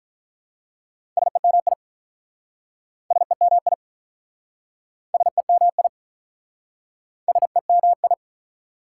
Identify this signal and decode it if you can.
Morse code, heard as a keyed tone, about 43 wpm, 700 Hz. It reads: SEMI SEMI SEMI SEMI
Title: Speed-Racing Prefixes and Suffixes 35wpm